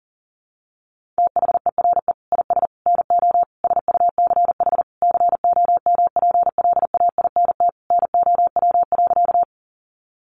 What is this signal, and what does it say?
Morse code, heard as a keyed tone, about 37 wpm, 700 Hz. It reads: THERE IS NO SUCH COMPLAINT NOW.
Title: Sentences from Top 1400 Words 40wpm